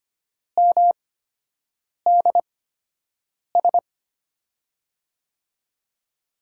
Morse code, heard as a keyed tone, about 25 wpm, spaced as for 7 wpm, 700 Hz. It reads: MDS
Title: USA Call Signs - 5 Characters - 8x Character Spacing - 25wpm